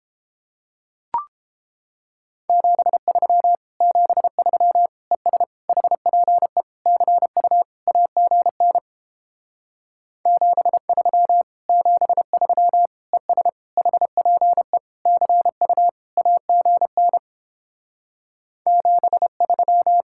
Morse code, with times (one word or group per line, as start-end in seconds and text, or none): 2.5-3.6 73
3.8-4.9 73
5.1-5.4 ES
5.7-6.6 HPE
6.9-7.6 CU
7.9-8.8 AGN
10.3-11.4 73
11.7-12.9 73
13.1-13.5 ES
13.8-14.8 HPE
15.1-15.9 CU
16.2-17.2 AGN
18.7-20.0 73